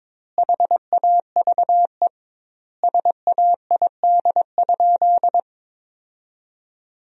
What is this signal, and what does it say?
Morse code, heard as a keyed tone, about 22 wpm, 700 Hz. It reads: HAVE SAID?
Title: 2x Word-Spacing - Sentences from Top 100 Words - 22wpm